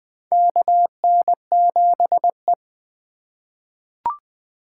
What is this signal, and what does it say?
Morse code, heard as a keyed tone, about 20 wpm, 700 Hz. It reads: KN7E
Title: USA Call Signs - 3x Character Spacing - 20wpm